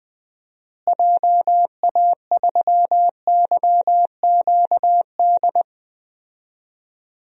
Morse code, heard as a keyed tone, about 20 wpm, 700 Hz. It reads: JA3YQD
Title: DXCC-100 Call Signs - 6 Characters - Repeated 3x - 20wpm